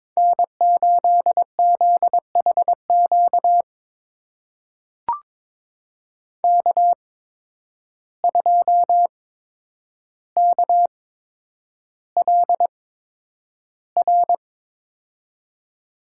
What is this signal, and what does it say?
Morse code, heard as a keyed tone, about 22 wpm, spaced as for 9 wpm, 700 Hz. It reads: N8ZHQ K2KLR
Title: USA Call Signs - 5 Characters - 8x Character Spacing - 22wpm